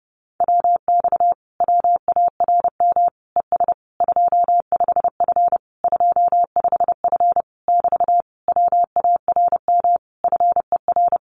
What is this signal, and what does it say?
Morse code, heard as a keyed tone, about 30 wpm, 700 Hz. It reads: WX WARM ES 25F 25F <BT> WARM FER
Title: Common QSO Phrases Encore 30wpm